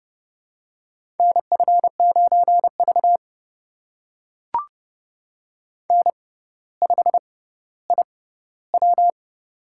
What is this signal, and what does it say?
Morse code, heard as a keyed tone, about 30 wpm, spaced as for 15 wpm, 700 Hz. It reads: NF9V N5IW
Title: USA Call Signs - 4 Characters - 6x Character Spacing - 30wpm